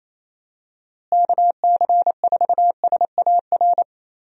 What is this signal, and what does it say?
Morse code, heard as a keyed tone, about 28 wpm, 700 Hz. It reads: KC4SAR
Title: USA Call Signs - 6 Characters - Repeated 2x - 28wpm